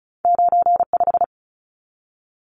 Morse code, heard as a keyed tone, about 36 wpm, 700 Hz. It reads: 95
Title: Numbers - 2-Digit 35wpm